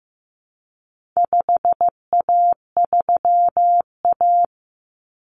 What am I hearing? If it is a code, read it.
Morse code, heard as a keyed tone, about 15 wpm, 700 Hz. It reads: 5A3A
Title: Common Cut Numbers - 4-Digit 15wpm